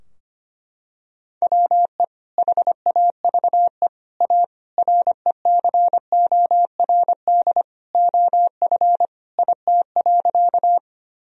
Morse code, heard as a keyed tone, about 25 wpm, 700 Hz. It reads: WE HAVE A RECORD OF IT.
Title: Sentences from Top 1000 Words - Review Part 3 - 25wpm